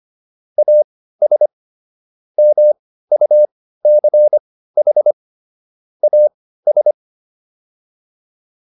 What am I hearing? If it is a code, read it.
Morse code, heard as a keyed tone, about 25 wpm, spaced as for 15 wpm, 600 Hz. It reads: AS MUCH AS
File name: WD204 15